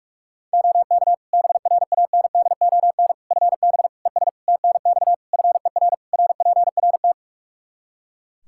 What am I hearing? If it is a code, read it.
Morse code, heard as a keyed tone, about 46 wpm, 700 Hz. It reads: OK BRANDON FB ES TNX FER RPRT
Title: K1USN SST QSO Practice 45wpm